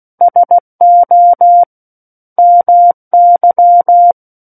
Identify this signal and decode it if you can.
Morse code, heard as a keyed tone, about 16 wpm, 700 Hz. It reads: SO MY
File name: Book One / 07-Chapter 7 - 16 WPM